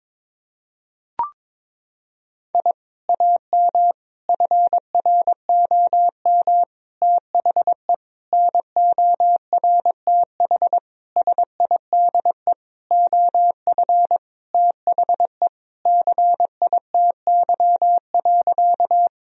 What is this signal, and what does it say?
Morse code, heard as a keyed tone, about 22 wpm, 700 Hz. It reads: I AM FROM THE NORTH SIDE OF THE CITY.
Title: Sentences from Top 1000 Words - Review Part 1 - 22wpm